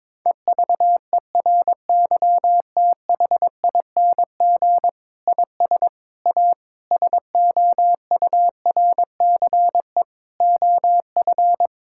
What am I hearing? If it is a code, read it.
Morse code, heard as a keyed tone, about 22 wpm, 700 Hz. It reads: EVERYTHING IS A SOURCE OF
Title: Sentences from Top 1000 Words - Review Part 5 - 22wpm